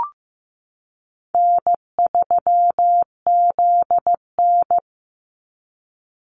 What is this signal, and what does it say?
Morse code, heard as a keyed tone, about 15 wpm, 700 Hz. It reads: N3ZN
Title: Easy Call Signs - 15wpm